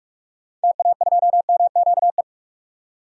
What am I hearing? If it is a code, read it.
Morse code, heard as a keyed tone, about 46 wpm, 700 Hz. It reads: TA2MQE
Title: DXCC-100 Call Signs - 6 Characters - 45wpm